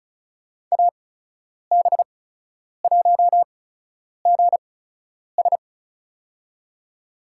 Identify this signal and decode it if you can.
Morse code, heard as a keyed tone, about 36 wpm, spaced as for 10 wpm, 700 Hz. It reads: AB1GS
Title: USA Call Signs - 5 Characters - 8x Character Spacing - 35wpm